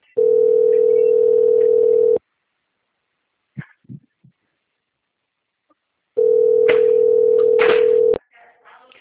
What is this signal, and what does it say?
Ringback tone (North America), 440 + 480 Hz.